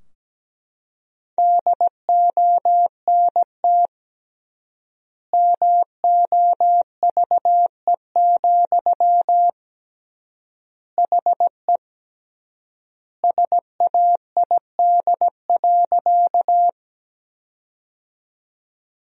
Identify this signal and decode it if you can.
Morse code, heard as a keyed tone, about 17 wpm, 700 Hz. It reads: DONT MOVE, HE SAID.
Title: 3x Word-Spacing - Sentences from Top 300 Words - 17wpm